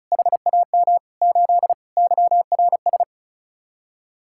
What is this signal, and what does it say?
Morse code, heard as a keyed tone, about 45 wpm, 700 Hz. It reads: HAM 8 YRS